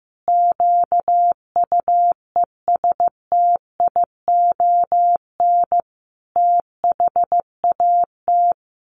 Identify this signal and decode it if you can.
Morse code, heard as a keyed tone, about 15 wpm, 700 Hz. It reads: QUESTION THAT
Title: Sets of 5 Words Encore 15wpm